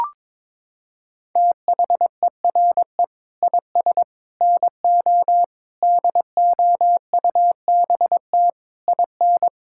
Morse code, heard as a keyed tone, about 22 wpm, 700 Hz. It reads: THERE IS NO DOUBT IN
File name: Sets of 9 Words - 22wpm